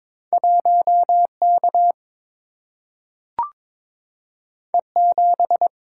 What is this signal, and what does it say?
Morse code, heard as a keyed tone, about 22 wpm, 700 Hz. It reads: 1K E7
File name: practice / 22wpm / Two Letter-Number Combinations - 22wpm